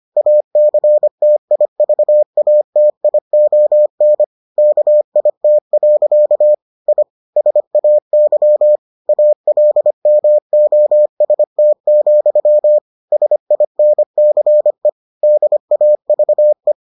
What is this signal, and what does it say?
Morse code, heard as a keyed tone, about 25 wpm, 600 Hz. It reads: ACTIVATION KIT. I SAY ALMOST, SINCE DAVE